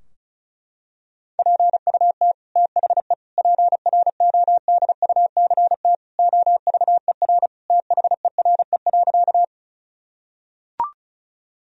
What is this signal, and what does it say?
Morse code, heard as a keyed tone, about 34 wpm, 700 Hz. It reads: PUT THE PRODUCT OVER THERE.
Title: Sentences from Top 1000 Words - Review Part 2 - 35wpm